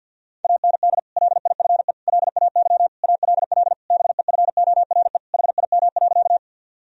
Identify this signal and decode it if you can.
Morse code, heard as a keyed tone, about 52 wpm, 700 Hz. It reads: AND LIFE LAY ALL BEFORE HIM.
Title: Sentences from Top 1000 Words - Review Part 2 - 50wpm